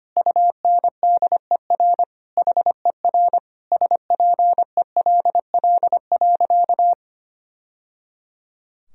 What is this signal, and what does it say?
Morse code, heard as a keyed tone, about 25 wpm, 700 Hz. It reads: UNDER HER SPELL.
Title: Sentences from Top 1000 Words - Review Part 1 - 25wpm